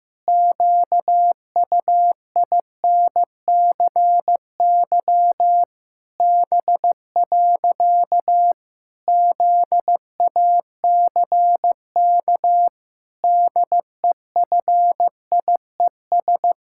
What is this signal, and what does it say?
Morse code, heard as a keyed tone, about 15 wpm, 700 Hz. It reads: QUINCY B. ZACK DEFIES
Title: Pangram Challenge v2 - 15wpm